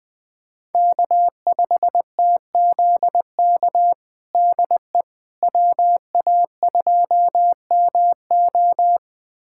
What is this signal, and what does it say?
Morse code, heard as a keyed tone, about 20 wpm, 700 Hz. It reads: K5TZK DE WA2MO
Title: USA Callsign DE Callsign - 5 Characters - 20wpm